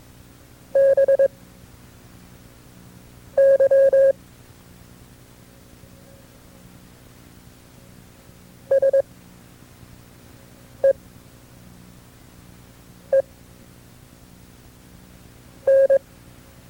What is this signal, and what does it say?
Morse code, heard as a keyed tone, about 22 wpm, spaced as for 4 wpm, 540 Hz. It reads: BY SEEN
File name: FreeFiles / CW-practice / KB6MT-tape2-side2